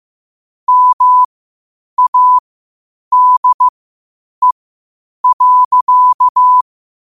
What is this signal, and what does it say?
Morse code, heard as a keyed tone, about 15 wpm, spaced as for 8 wpm, 1000 Hz. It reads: MADE.
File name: qotdfiles / 8wpm